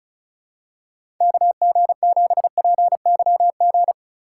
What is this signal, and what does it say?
Morse code, heard as a keyed tone, about 36 wpm, 700 Hz. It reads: KG7PYG
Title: USA Call Signs - 3x Character Spacing - 35wpm